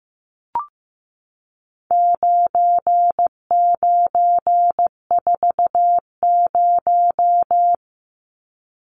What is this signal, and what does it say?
Morse code, heard as a keyed tone, about 15 wpm, 700 Hz. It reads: 9940
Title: Numbers - 4-Digit 15wpm